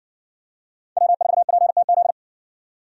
Morse code, heard as a keyed tone, about 53 wpm, 700 Hz. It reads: W4PTZ